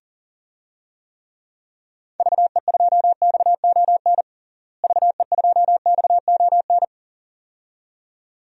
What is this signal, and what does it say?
Morse code, heard as a keyed tone, about 40 wpm, 700 Hz. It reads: VE2XON VE2XON
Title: DXCC-100 Call Signs - 6 Characters - Repeated 2x - 40wpm